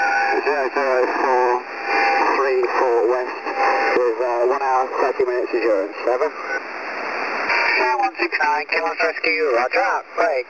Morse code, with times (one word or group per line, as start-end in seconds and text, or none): none